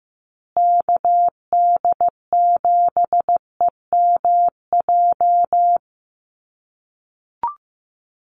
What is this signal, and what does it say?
Morse code, heard as a keyed tone, about 15 wpm, 700 Hz. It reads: KD7EMJ
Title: USA Call Signs Repeated 3x 15wpm